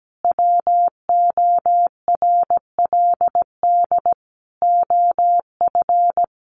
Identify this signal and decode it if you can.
Morse code, heard as a keyed tone, about 17 wpm, 700 Hz. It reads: WORLD OF
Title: Sentences from Top 1500 Words 17wpm